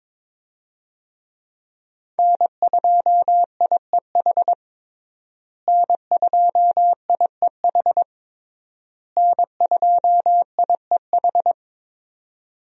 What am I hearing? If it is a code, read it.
Morse code, heard as a keyed tone, about 21 wpm, 700 Hz. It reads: N2IEH N2IEH N2IEH
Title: USA Call Signs - 5 Characters - Repeated 3x - 22wpm